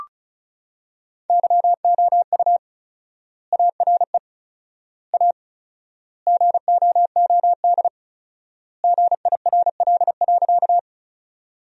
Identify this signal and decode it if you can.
Morse code, heard as a keyed tone, about 36 wpm, 700 Hz. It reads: YOU ARE A GOOD GIRL.
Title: 4x Word-Spacing - Sentences from Top 400 Words - 35wpm